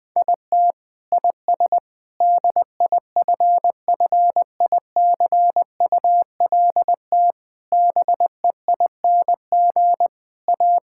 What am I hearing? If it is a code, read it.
Morse code, heard as a keyed tone, about 20 wpm, 700 Hz. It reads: IT IS DIFFICULT BEING A
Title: Sentences from Top 1000 Words - Review Part 5 - 20wpm